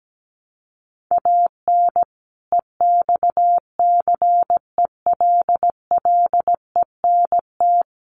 Morse code, heard as a keyed tone, about 17 wpm, 700 Hz. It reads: AN EXCELLENT
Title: Sentences from Top 1200 Words 17wpm